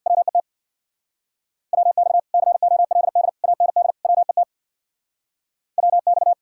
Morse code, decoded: WET WX COLD AND WET WX

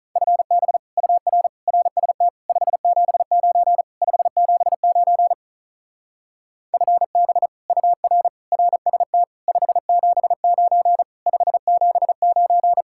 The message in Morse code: FB UR RST 579 579 FB UR RST 579 579

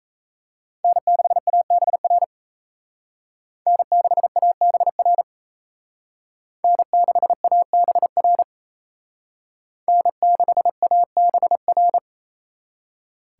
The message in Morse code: N6ABR N6ABR N6ABR N6ABR

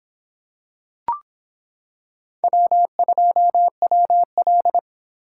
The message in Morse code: W2WL

